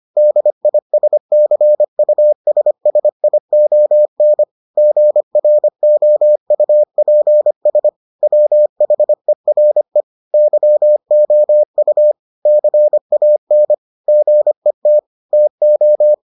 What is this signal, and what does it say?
Morse code, heard as a keyed tone, about 25 wpm, 600 Hz. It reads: DISCUSSION GROUPS WHERE YOU CAN GET TO